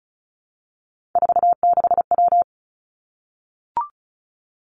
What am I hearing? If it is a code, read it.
Morse code, heard as a keyed tone, about 36 wpm, 700 Hz. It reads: 46W
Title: Three Challenging Letter-Number Combinations 35wpm